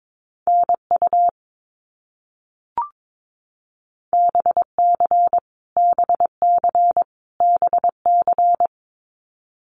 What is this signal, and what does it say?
Morse code, heard as a keyed tone, about 22 wpm, 700 Hz. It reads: NU BC BC BC
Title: Canadian Provinces and Territories - Repeated 3x - 22wpm